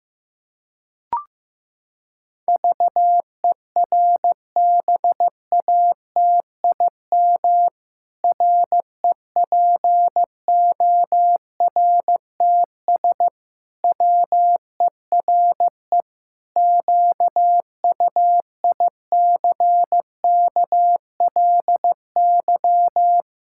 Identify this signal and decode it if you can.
Morse code, heard as a keyed tone, about 15 wpm, 700 Hz. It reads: VERBATIM REPORTS WERE QUICKLY
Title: Pangram Challenge v2 - 15wpm